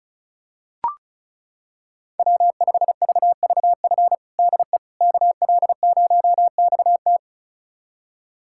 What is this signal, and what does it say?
Morse code, heard as a keyed tone, about 36 wpm, 700 Hz. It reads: W5VVF DE KL0XT
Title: USA Callsign DE Callsign - 5 Characters - 35wpm